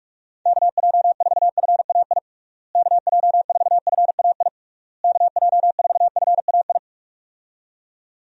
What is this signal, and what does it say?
Morse code, heard as a keyed tone, about 46 wpm, 700 Hz. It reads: KJ4FAI KJ4FAI KJ4FAI
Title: USA Call Signs - 6 Characters - Repeated 3x - 45wpm